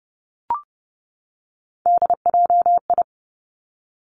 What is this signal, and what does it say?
Morse code, heard as a keyed tone, about 30 wpm, 700 Hz. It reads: DJI
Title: Three Challenging Letter-Number Combinations 30wpm